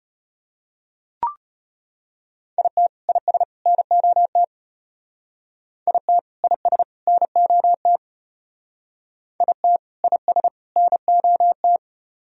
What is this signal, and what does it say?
Morse code, heard as a keyed tone, about 32 wpm, 700 Hz. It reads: IT IS NOT IT IS NOT IT IS NOT